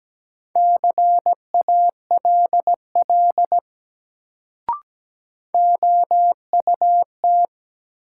Morse code, heard as a keyed tone, about 17 wpm, 700 Hz. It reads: CALL OUT
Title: Sentences from Top 100 Words - New Word Drill - 17wpm